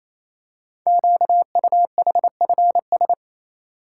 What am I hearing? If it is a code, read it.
Morse code, heard as a keyed tone, about 29 wpm, 700 Hz. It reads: QUHFS
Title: Code Groups - 5 Letters - 28wpm